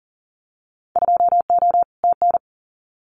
Morse code, heard as a keyed tone, about 40 wpm, 700 Hz. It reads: 2O TN
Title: Winter Field Day Exchanges 40wpm